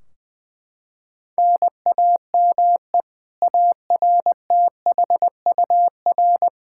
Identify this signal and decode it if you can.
Morse code, heard as a keyed tone, about 20 wpm, 700 Hz. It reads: NAME ARTHUR
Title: Practice QSO - Exchange 1 - Beginning - 20wpm